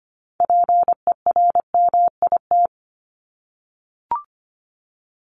Morse code, heard as a keyed tone, about 25 wpm, 700 Hz. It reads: PERMIT
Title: Top 1751-2000 Words 25wpm